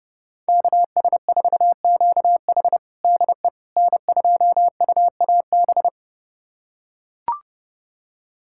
Morse code, heard as a keyed tone, about 30 wpm, 700 Hz. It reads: KS4QH DE N2UAB
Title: USA Callsign DE Callsign - 5 Characters - 30wpm